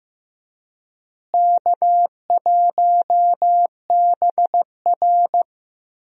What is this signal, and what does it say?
Morse code, heard as a keyed tone, about 15 wpm, 700 Hz. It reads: K1BR
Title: USA Call Signs - 4 Characters - Repeated 3x - 15wpm